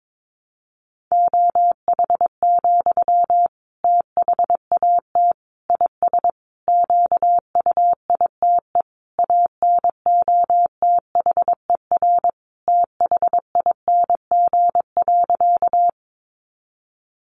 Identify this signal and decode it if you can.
Morse code, heard as a keyed tone, about 22 wpm, 700 Hz. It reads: OH, THAT IS QUITE ANOTHER THING.
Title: Sentences from Top 700 Words 22wpm